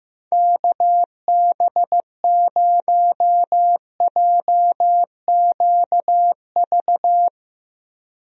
Morse code, heard as a keyed tone, about 15 wpm, 700 Hz. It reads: KB0JQV
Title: USA Call Signs - 6 Characters - Repeated 3x - 15wpm